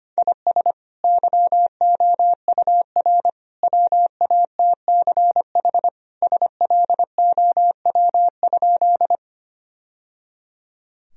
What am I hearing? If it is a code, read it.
Morse code, heard as a keyed tone, about 25 wpm, 700 Hz. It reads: IS YOUR WATCH SLOW?
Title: Sentences from Top 1000 Words - Review Part 2 - 25wpm